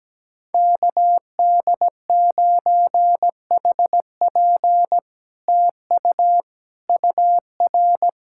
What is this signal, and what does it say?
Morse code, heard as a keyed tone, about 17 wpm, 700 Hz. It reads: KD9HP TU UR
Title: SOTA QSO Practice 17wpm